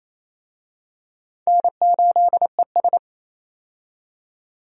Morse code, heard as a keyed tone, about 28 wpm, 700 Hz. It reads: N8ES
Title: USA Call Signs - 4 Characters - 28wpm